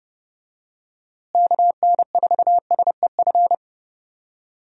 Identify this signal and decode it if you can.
Morse code, heard as a keyed tone, about 30 wpm, 700 Hz. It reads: KN4SEF
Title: USA Call Signs - 6 Characters - Repeated 2x - 30wpm